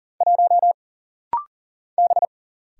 Morse code, heard as a keyed tone, about 44 wpm, 700 Hz. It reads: 1 B
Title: Single Letter-Number Alt - Rapid-Fire 40wpm